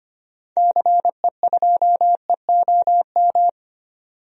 Morse code, heard as a keyed tone, about 25 wpm, 700 Hz. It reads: CE2EOM
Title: DXCC-100 Call Signs - 6 Characters - 25wpm